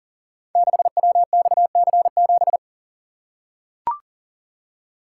Morse code, heard as a keyed tone, about 38 wpm, 690 Hz. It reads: 6WXC7
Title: Code Groups - 5 Letter:Numbers - 40wpm